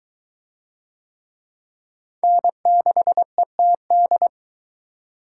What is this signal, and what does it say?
Morse code, heard as a keyed tone, about 20 wpm, 700 Hz. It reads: N6ETD